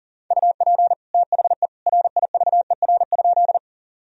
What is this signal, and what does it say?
Morse code, heard as a keyed tone, about 40 wpm, 700 Hz. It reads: UP THE RIVER?
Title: Sentences from Top 1000 Words - Review Part 1 - 40wpm